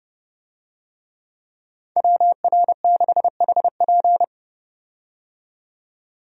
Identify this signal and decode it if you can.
Morse code, heard as a keyed tone, about 32 wpm, 700 Hz. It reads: WR6HP
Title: USA Call Signs - 5 Characters - 30wpm